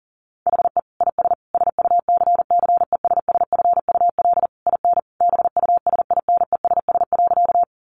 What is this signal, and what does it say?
Morse code, heard as a keyed tone, about 40 wpm, 700 Hz. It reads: HE IS SUCCESSFUL IN BUSINESS.
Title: Sentences from Top 900 Words 40wpm